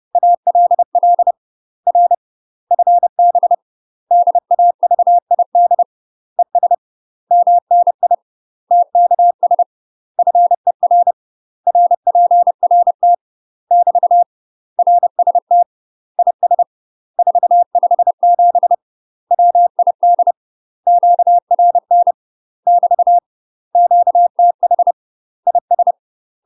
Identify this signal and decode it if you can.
Morse code, heard as a keyed tone, about 30 wpm, 700 Hz. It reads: ALL R FB DAVID ES MNI TKS FER RPRT <BT> RST IS 457 WID QRN <BT> QTH IS
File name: UK MASTER QSOs TRAINING/30WPM